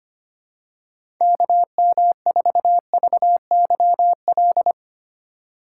Morse code, heard as a keyed tone, about 25 wpm, 700 Hz. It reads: KM4VYL
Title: USA Call Signs - 6 Characters - 3x Character Spacing - 25wpm